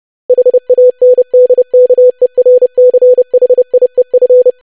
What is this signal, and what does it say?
Morse code, heard as a keyed tone, about 30 wpm, 500 Hz. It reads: HANDKERCHIEF
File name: lesson / web / longwords-25wpm